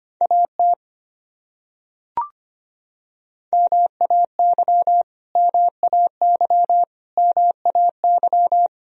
Morse code, heard as a keyed tone, about 25 wpm, 700 Hz. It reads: AT MAY MAY MAY
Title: Top 100 Words Repeated 25wpm